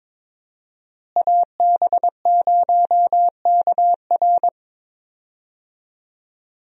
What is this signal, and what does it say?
Morse code, heard as a keyed tone, about 22 wpm, 700 Hz. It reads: AB0KR
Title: USA Call Signs - 5 Characters - Repeated 2x - 22wpm